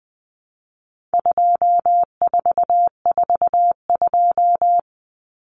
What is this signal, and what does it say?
Morse code, heard as a keyed tone, about 20 wpm, 700 Hz. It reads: 2442
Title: Numbers - 4-Digit 20wpm